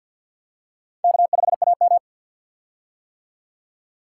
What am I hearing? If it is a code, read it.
Morse code, heard as a keyed tone, about 53 wpm, 700 Hz. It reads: K5AM
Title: K1USN SST QSO Practice - 50wpm